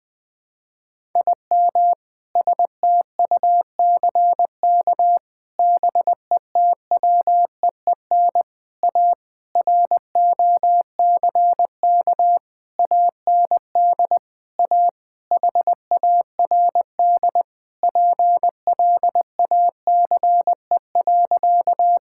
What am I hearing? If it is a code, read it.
Morse code, heard as a keyed tone, about 21 wpm, 700 Hz. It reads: IM STUCK BETWEEN A ROCK AND A HARD PLACE.
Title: Sentences with 7-Word Phrases - Spoken Beginning - 20wpm